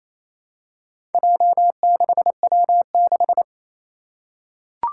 Morse code, heard as a keyed tone, about 27 wpm, 700 Hz. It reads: J6W6
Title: Four Challenging Letter-Number Combinations 28wpm